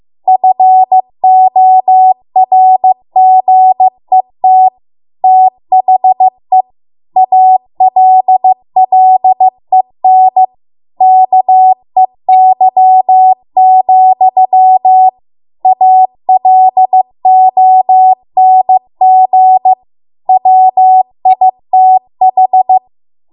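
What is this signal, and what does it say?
Morse code, heard as a keyed tone, about 15 wpm, 750 Hz. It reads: FORGET THE ALLEN KEY, ALONG WITH
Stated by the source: Morse code practice at 15 wpm from the national amateur radio association's code archive.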